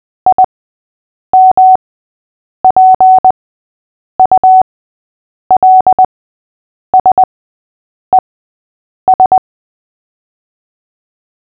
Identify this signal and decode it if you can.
Morse code, heard as a keyed tone, about 20 wpm, spaced as for 8 wpm, 750 Hz. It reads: IMPULSES
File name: cw / fists / Short-Story-6-8wpm